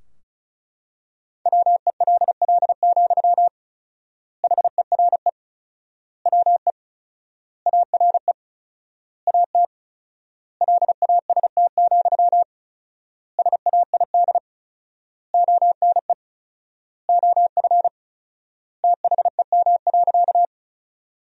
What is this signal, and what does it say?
Morse code, heard as a keyed tone, about 34 wpm, 700 Hz. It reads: WELL, HERE WE ARE AT LAST, SAID ONE OF THEM.
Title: 4x Word-Spacing - Sentences from Top 300 Words - 35wpm